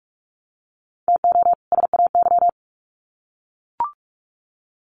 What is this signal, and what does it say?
Morse code, heard as a keyed tone, about 44 wpm, 700 Hz. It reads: TO SAY
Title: Sets of 2 Words 45wpm